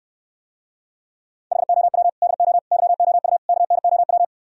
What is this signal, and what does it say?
Morse code, heard as a keyed tone, about 68 wpm, 700 Hz. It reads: HOW DO YOU KNOW